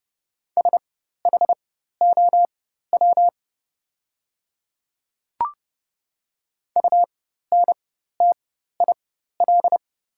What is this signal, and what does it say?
Morse code, heard as a keyed tone, about 30 wpm, spaced as for 14 wpm, 690 Hz. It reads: SHOW UNTIL
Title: 4x Character Spacing - Top 300 Words - 30wpm